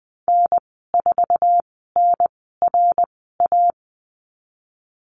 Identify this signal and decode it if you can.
Morse code, heard as a keyed tone, about 20 wpm, spaced as for 15 wpm, 700 Hz. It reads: N4NRA
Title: 055 Lesson 10g - Letter r - Call Signs 20wpm